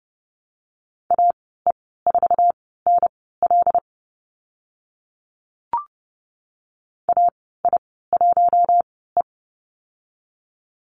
Morse code, heard as a keyed tone, about 30 wpm, spaced as for 17 wpm, 700 Hz. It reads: AE4NL AI1E
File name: Lesson 14g - Number 2 - Call Signs 30wpm